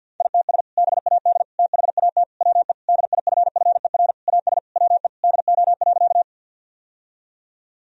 Morse code, heard as a keyed tone, about 51 wpm, 700 Hz. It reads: ITS BAD THAT WE DIFFER AS WE DO.
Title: Sentences from Top 1000 Words - Review Part 1 - 50wpm